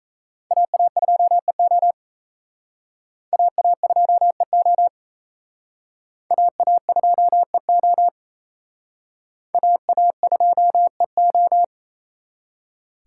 Morse code, heard as a keyed tone, about 35 wpm, 700 Hz. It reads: AA2EO AA2EO AA2EO AA2EO